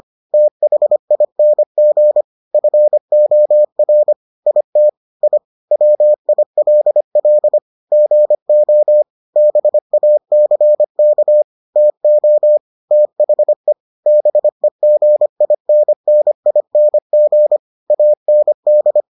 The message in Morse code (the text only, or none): THING FOR IT I WILL GO BACK TO THE BEGINNING AND